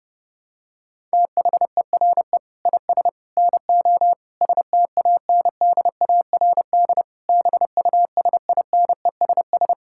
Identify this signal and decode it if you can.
Morse code, heard as a keyed tone, about 30 wpm, 700 Hz. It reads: THERE IS NO STANDARD BUSINESS
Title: Sentences from Top 1000 Words - Review Part 6 - 30wpm